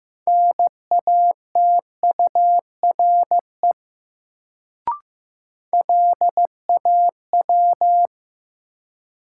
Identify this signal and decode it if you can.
Morse code, heard as a keyed tone, about 15 wpm, 700 Hz. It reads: NATURE LAW
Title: Sentences from Top 600 Words - New Word Drill - 15wpm